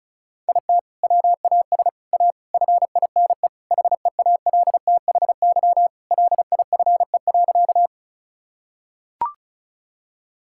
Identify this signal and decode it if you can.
Morse code, heard as a keyed tone, about 34 wpm, 700 Hz. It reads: IT WAS A FINE HEALTHY LIFE.
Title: Sentences from Top 1000 Words - Review Part 5 - 35wpm